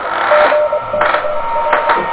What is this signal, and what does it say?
Morse code, heard as a keyed tone, about 22 wpm, 600 Hz, with noise. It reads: GKR
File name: OK QRP 2023 / OK QRP 2023